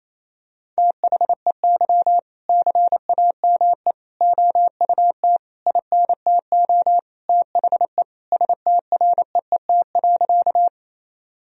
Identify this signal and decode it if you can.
Morse code, heard as a keyed tone, about 27 wpm, 700 Hz. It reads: THEY CAME OUT INTO THE STREET.
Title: Sentences from Top 1000 Words - Review Part 3 - 28wpm